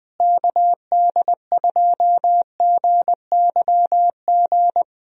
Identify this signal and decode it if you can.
Morse code, heard as a keyed tone, about 20 wpm, 700 Hz. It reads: KD2GYG
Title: USA Call Signs - 6 Characters - Repeated 2x - 20wpm